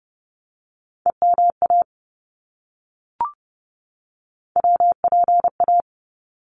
Morse code, heard as a keyed tone, about 30 wpm, 700 Hz. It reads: EMA WPA
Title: ARRL Sections 30wpm